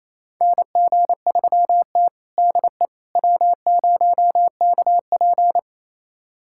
Morse code, heard as a keyed tone, about 28 wpm, 700 Hz. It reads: NG3T DE W0KP
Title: USA Callsign DE Callsign - 4 Characters - 28wpm